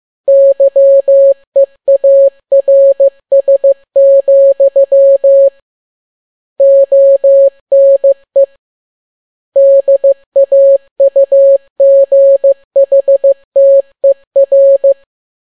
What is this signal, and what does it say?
Morse code, heard as a keyed tone, about 15 wpm, 550 Hz. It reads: YEARS, ONE DAUGHTER